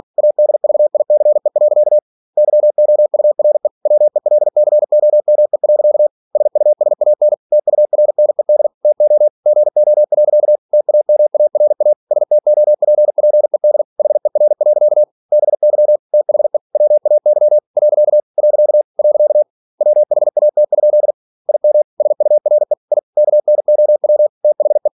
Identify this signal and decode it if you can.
Morse code, heard as a keyed tone, about 46 wpm, 600 Hz. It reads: ADVICE. YOURE WELCOME. SUSAN TURNED TO GO. TAMARA STOPPED HER. BY THE WAY . . WHAT? IM SURE I KNOW THE